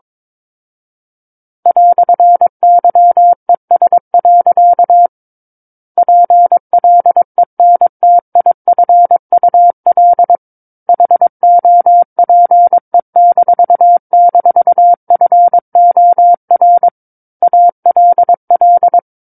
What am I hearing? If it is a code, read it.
Morse code, heard as a keyed tone, about 22 wpm, 700 Hz. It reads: "YES. PLENTIFUL HOPE--FOR ALL
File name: Book One / 13-Chapter 13 - 22 WPM